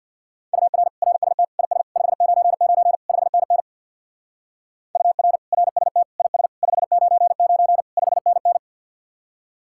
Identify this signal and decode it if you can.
Morse code, heard as a keyed tone, about 58 wpm, 700 Hz. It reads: UR RST IS 599 5NN UR RST IS 599 5NN
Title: Speed-Racing Common QSO Phrases - 50wpm